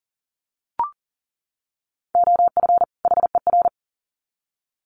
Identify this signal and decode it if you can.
Morse code, heard as a keyed tone, about 40 wpm, 700 Hz. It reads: OF HER
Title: Sets of 2 Words 40wpm